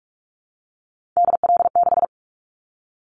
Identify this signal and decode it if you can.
Morse code, heard as a keyed tone, about 46 wpm, 700 Hz. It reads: DL6
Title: Callsign Prefixes for Top 50 DXCC - 45wpm